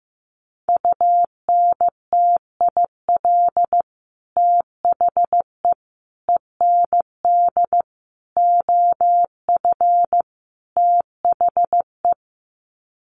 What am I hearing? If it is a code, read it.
Morse code, heard as a keyed tone, about 15 wpm, 700 Hz. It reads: UNTIL THE END OF THE
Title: Sets of 5 Words Encore 15wpm